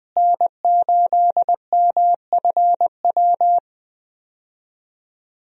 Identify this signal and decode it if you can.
Morse code, heard as a keyed tone, about 20 wpm, 700 Hz. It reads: N8MFW
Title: USA Call Signs - 5 Characters - Repeated 3x - 20wpm